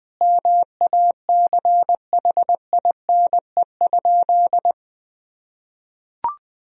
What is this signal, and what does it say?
Morse code, heard as a keyed tone, about 20 wpm, 700 Hz. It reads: MACHINE?
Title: Sentences from Top 1000 Words - Review Part 2 - 20wpm